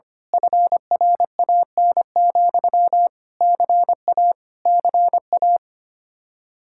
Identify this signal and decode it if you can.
Morse code, heard as a keyed tone, about 24 wpm, 700 Hz. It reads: FRAN, CA CA
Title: Speed-Racing Common QSO Phrases Encore - 25wpm